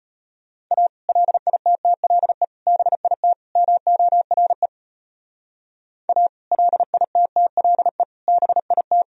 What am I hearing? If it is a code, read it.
Morse code, heard as a keyed tone, about 39 wpm, 700 Hz. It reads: A LITTLE BIT MORE A LITTLE BIT